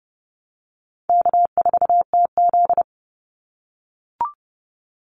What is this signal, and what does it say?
Morse code, heard as a keyed tone, about 30 wpm, 700 Hz. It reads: K4TZ
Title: Easy Call Signs - 30wpm